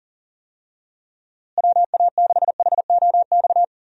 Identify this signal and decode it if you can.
Morse code, heard as a keyed tone, about 40 wpm, 700 Hz. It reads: WA6HOX
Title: USA Call Signs - 6 Characters - Repeated 2x - 40wpm